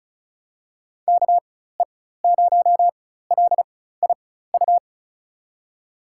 Morse code, heard as a keyed tone, about 36 wpm, spaced as for 16 wpm, 700 Hz. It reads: KE0LIU
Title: USA Call Signs - 6 Characters - 4x Character Spacing - 35wpm